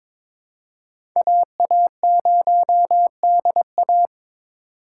Morse code, heard as a keyed tone, about 21 wpm, 700 Hz. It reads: AA0DA
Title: USA Call Signs - 5 Characters - 4x Character Spacing - 22wpm